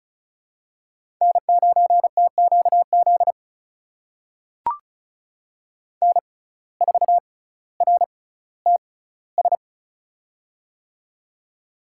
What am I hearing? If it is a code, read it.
Morse code, heard as a keyed tone, about 36 wpm, spaced as for 18 wpm, 700 Hz. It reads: N9TQZ N4RTS
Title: USA Call Signs - 5 Characters - 6x Character Spacing - 35wpm